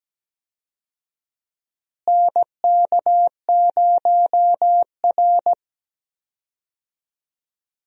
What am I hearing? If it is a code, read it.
Morse code, heard as a keyed tone, about 17 wpm, 700 Hz. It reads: NK0R
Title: USA Call Signs - 4 Characters - Repeated 2x - 17wpm